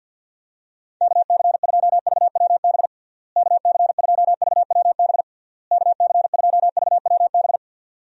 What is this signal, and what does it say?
Morse code, heard as a keyed tone, about 48 wpm, 700 Hz. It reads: KC2VWB KC2VWB KC2VWB
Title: USA Call Signs - 6 Characters - Repeated 3x - 50wpm